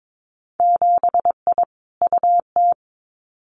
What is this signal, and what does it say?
Morse code, heard as a keyed tone, about 22 wpm, 700 Hz. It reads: 7I UT
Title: Winter Field Day Exchanges Encore 22wpm